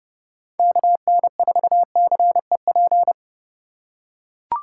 Morse code, heard as a keyed tone, about 30 wpm, 700 Hz. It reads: KN4CEP